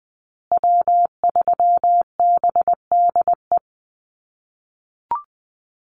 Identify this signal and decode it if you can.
Morse code, heard as a keyed tone, about 20 wpm, 700 Hz. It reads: W3BDE
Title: USA Call Signs Repeated 2x 20wpm